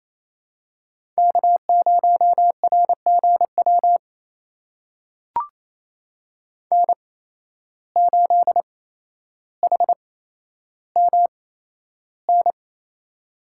Morse code, heard as a keyed tone, about 28 wpm, spaced as for 12 wpm, 700 Hz. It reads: K0RGW N8HMN
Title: USA Call Signs - 5 Characters - 8x Character Spacing - 28wpm